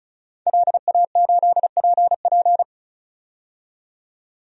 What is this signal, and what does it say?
Morse code, heard as a keyed tone, about 34 wpm, 700 Hz. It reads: LA8PP
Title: DXCC-100 Call Signs - 5 Characters - 35wpm